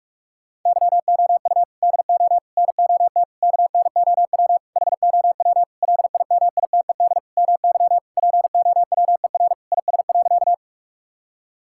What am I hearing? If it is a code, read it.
Morse code, heard as a keyed tone, about 41 wpm, 700 Hz. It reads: YOU DO NOT KNOW HOW LIMITED MY POWER IS.
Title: Sentences from Top 1000 Words - Review Part 6 - 45wpm